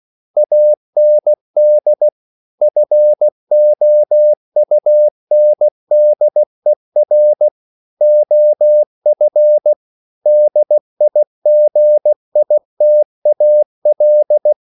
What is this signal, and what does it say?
Morse code, heard as a keyed tone, about 16 wpm, 600 Hz. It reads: AND FOUNDER OF DIGITAL
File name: CW / Practice / EnglishQuotations0416